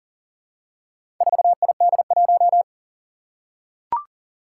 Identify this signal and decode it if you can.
Morse code, heard as a keyed tone, about 37 wpm, 700 Hz. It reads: 4ID1
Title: Four Challenging Letter-Number Combinations 40wpm